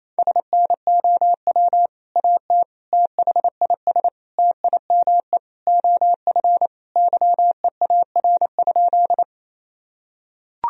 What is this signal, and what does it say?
Morse code, heard as a keyed tone, about 28 wpm, 700 Hz. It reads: SNOW AT THIS TIME OF YEAR?
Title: Sentences from Top 1000 Words - Review Part 3 - 28wpm